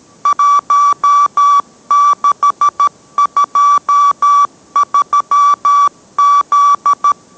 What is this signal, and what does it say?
Morse code, heard as a keyed tone, about 13 wpm, 1150 Hz. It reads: 1623Z